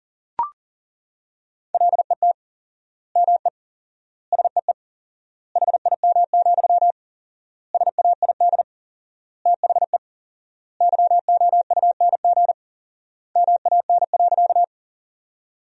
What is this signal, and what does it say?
Morse code, heard as a keyed tone, about 40 wpm, 700 Hz. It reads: LET ME SEE HIM, SAID THE YOUNG MAN.
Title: 4x Word-Spacing - Sentences from Top 400 Words - 40wpm